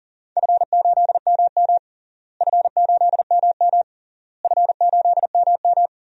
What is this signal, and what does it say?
Morse code, heard as a keyed tone, about 37 wpm, 700 Hz. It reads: F8MM F8MM F8MM
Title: DXCC-100 Call Signs - 4 Characters - Repeated 3x - 40wpm